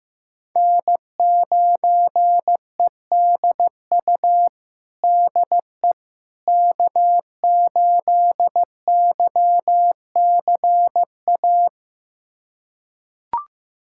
Morse code, heard as a keyed tone, about 15 wpm, 700 Hz. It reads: N9EDU DE K8YCA
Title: USA Callsign DE Callsign - 5 Characters - 15wpm